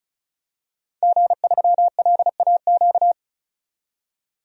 Morse code, heard as a keyed tone, about 34 wpm, 700 Hz. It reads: G3LAQ
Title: DXCC-100 Call Signs - 5 Characters - 35wpm